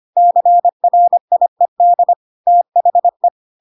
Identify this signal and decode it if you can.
Morse code, heard as a keyed tone, about 25 wpm, 700 Hz. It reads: CRIED THE
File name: Pinocchio / Pinocchio ch1 25 25 ik1hge